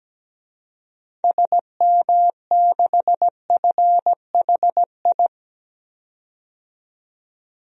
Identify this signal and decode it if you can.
Morse code, heard as a keyed tone, about 17 wpm, 700 Hz. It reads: SM6FHI
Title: DXCC-100 Call Signs - 6 Characters - 17wpm